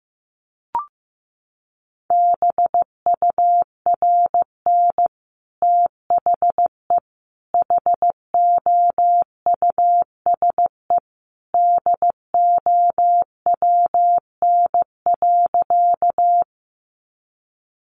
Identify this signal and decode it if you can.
Morse code, heard as a keyed tone, about 15 wpm, 700 Hz. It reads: BURN THE HOUSE DOWN.
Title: Sentences from Top 1300 Words 15wpm